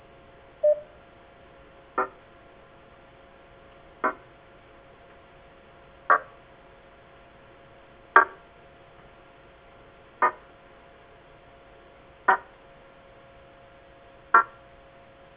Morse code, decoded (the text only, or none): none